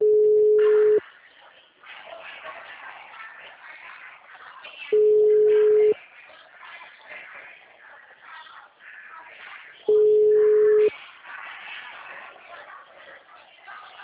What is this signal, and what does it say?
Ringback tone (Europe, 425 Hz), with speech.